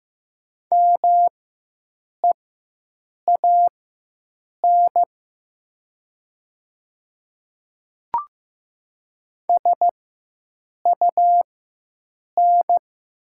Morse code, heard as a keyed tone, about 15 wpm, spaced as for 7 wpm, 700 Hz. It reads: MEAN SUN
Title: 4x Character Spacing - Top 300 Words - 15wpm